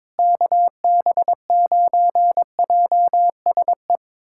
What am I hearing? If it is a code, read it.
Morse code, heard as a keyed tone, about 22 wpm, 700 Hz. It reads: KB9JSE
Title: USA Callsign DE Callsign 22wpm